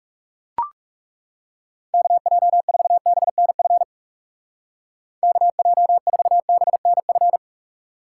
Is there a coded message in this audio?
Yes: Morse code from 1.9 to 7.4 s, heard as a keyed tone, about 38 wpm, 700 Hz. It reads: KJ4BNF KJ4BNF